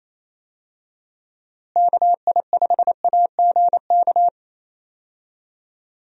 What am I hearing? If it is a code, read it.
Morse code, heard as a keyed tone, about 27 wpm, 700 Hz. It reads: KI5AGK
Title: USA Call Signs - 6 Characters - Repeated 2x - 28wpm